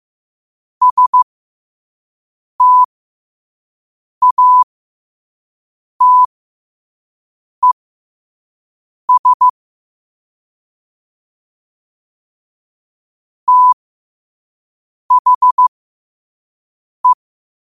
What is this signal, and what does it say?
Morse code, heard as a keyed tone, about 15 wpm, spaced as for 5 wpm, 1000 Hz. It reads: STATES THE
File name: qotdfiles / 5wpm